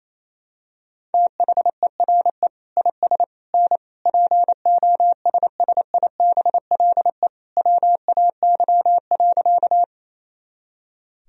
Morse code, heard as a keyed tone, about 28 wpm, 700 Hz. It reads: THERE IS N POSSIBLE WAY.
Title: Sentences from Top 500 Words and Abbreviations 28wpm